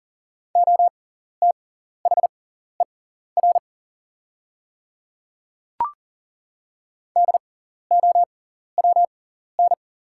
Morse code, heard as a keyed tone, about 39 wpm, spaced as for 14 wpm, 700 Hz. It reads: OTHER DOWN
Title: 6x Character Spacing - Top 100 Words - 40wpm